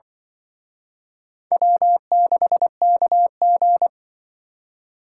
Morse code, heard as a keyed tone, about 23 wpm, 700 Hz. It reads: W6KG
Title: Speed Racing - USA 1x2 Call Signs 20wpm